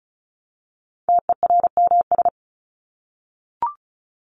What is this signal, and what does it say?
Morse code, heard as a keyed tone, about 34 wpm, 700 Hz. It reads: TERMS
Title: ES:S Suffix - Top 100 Words with Base Words 35wpm